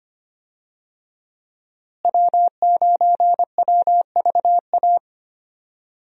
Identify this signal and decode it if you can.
Morse code, heard as a keyed tone, about 25 wpm, 700 Hz. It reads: W9WVA
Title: USA Call Signs - 5 Characters - Repeated 2x - 25wpm